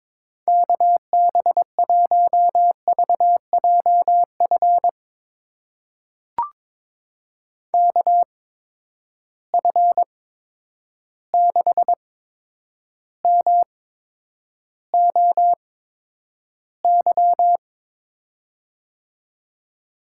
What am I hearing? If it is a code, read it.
Morse code, heard as a keyed tone, about 22 wpm, spaced as for 9 wpm, 700 Hz. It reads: KB1VJF KF6MOY
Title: USA Call Signs - 6 Characters - 8x Character Spacing - 22wpm